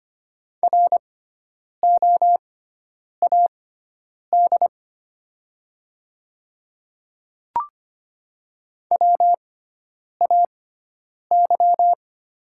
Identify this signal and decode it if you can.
Morse code, heard as a keyed tone, about 25 wpm, spaced as for 9 wpm, 700 Hz. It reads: ROAD WAY
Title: 6x Character Spacing - Top 500 Words - 25wpm